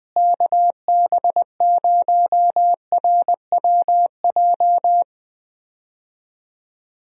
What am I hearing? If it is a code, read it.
Morse code, heard as a keyed tone, about 20 wpm, 700 Hz. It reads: KB0RWJ
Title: USA Call Signs - 6 Characters - Repeated 2x - 20wpm